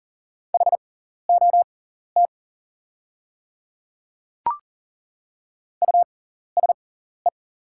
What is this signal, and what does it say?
Morse code, heard as a keyed tone, about 40 wpm, spaced as for 14 wpm, 700 Hz. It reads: HOT USE
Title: 6x Character Spacing - Top 100 Words - 40wpm